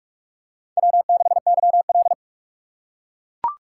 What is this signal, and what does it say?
Morse code, heard as a keyed tone, about 44 wpm, 690 Hz. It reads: W6YL
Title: USA Call Signs - 4 Characters - 4x Character Spacing - 45wpm